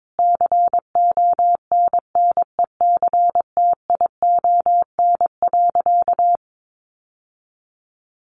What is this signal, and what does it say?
Morse code, heard as a keyed tone, about 22 wpm, 700 Hz. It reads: CONNECTION.
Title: Sentences from Top 1300 Words 22wpm